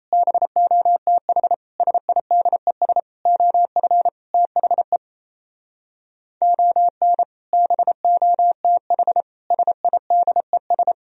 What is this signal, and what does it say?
Morse code, heard as a keyed tone, about 36 wpm, 700 Hz. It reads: BOTH SIDES OF THE ON BOTH SIDES